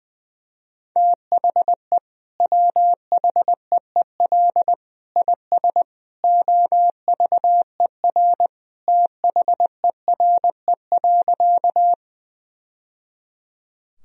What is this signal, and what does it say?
Morse code, heard as a keyed tone, about 20 wpm, 700 Hz. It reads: THE WHEEL IS OVER THERE.
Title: Sentences from Top 1000 Words - Review Part 3 - 20wpm